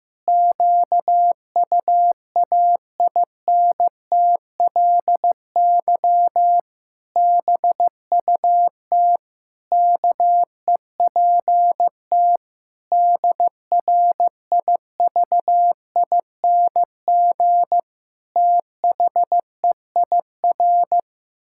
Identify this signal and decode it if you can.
Morse code, heard as a keyed tone, about 14 wpm, 700 Hz. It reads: QUAINTLY BUT KEPT DRIVING THEIR
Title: Pangram Challenge v2 - 15wpm